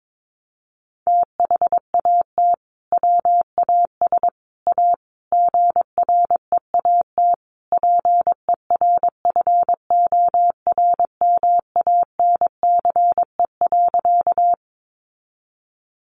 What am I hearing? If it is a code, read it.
Morse code, heard as a keyed tone, about 22 wpm, 700 Hz. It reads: THAT WAS A GREAT PERFORMANCE.
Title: Sentences from Top 1000 Words 22wpm